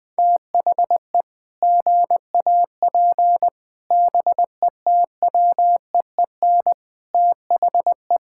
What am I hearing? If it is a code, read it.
Morse code, heard as a keyed tone, about 20 wpm, 700 Hz. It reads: THE GAP BETWEEN THE
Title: Sets of 8 Words - 20wpm